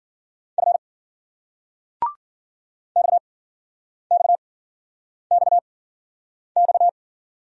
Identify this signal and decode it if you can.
Morse code, heard as a keyed tone, about 46 wpm, spaced as for 9 wpm, 700 Hz. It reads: F XXXX
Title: Speed-Racing Single Letters 40wpm